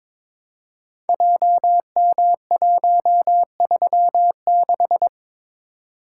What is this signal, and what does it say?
Morse code, heard as a keyed tone, about 22 wpm, 700 Hz. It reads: JM136
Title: Code Groups - 5 Letter:Numbers - 22wpm